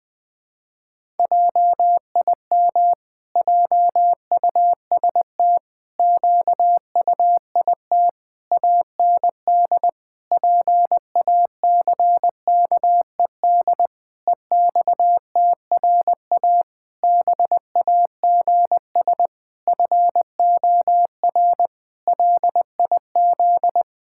Morse code, heard as a keyed tone, about 21 wpm, 700 Hz. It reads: JIM JUST QUIT AND PACKED EXTRA BAGS FOR LIZ
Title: Pangram Challenge v2 - 20wpm